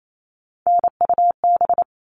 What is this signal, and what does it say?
Morse code, heard as a keyed tone, about 28 wpm, 700 Hz. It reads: NUB
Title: All Cut Numbers - 3-Digit 28wpm